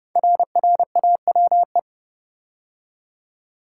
Morse code, heard as a keyed tone, about 30 wpm, 700 Hz. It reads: RRAWE
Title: Code Groups - 5 Letters - 30wpm